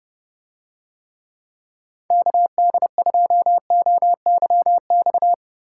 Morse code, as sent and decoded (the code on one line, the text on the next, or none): -.- -.. ..--- --- -.-- -..-
KD2OYX